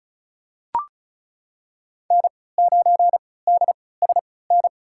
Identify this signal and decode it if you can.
Morse code, heard as a keyed tone, about 36 wpm, spaced as for 20 wpm, 700 Hz. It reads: N9DSN
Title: USA Call Signs - 5 Characters - 3x Character Spacing - 35wpm